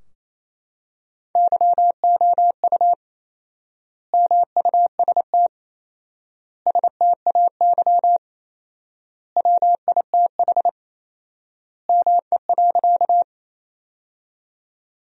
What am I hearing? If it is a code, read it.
Morse code, heard as a keyed tone, about 28 wpm, 700 Hz. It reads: YOU MUST STAY WITH ME.
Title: 4x Word-Spacing - Sentences from Top 500 Words - 28wpm